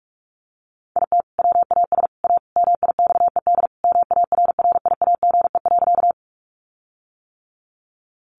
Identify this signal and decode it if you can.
Morse code, heard as a keyed tone, about 44 wpm, 700 Hz. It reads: IT WAS A MIXED MARRIAGE.
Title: Sentences from Top 1500 Words 45wpm